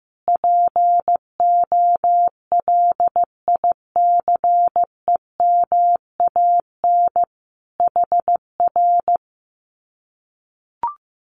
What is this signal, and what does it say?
Morse code, heard as a keyed tone, about 15 wpm, 700 Hz. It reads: POLICEMAN HR
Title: Common QSO Phrases Encore 15wpm